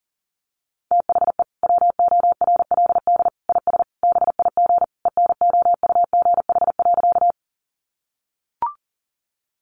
Morse code, heard as a keyed tone, about 40 wpm, 700 Hz. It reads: THE WORLD IS BIG ENOUGH.
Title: Sentences from Top 400 Words 40wpm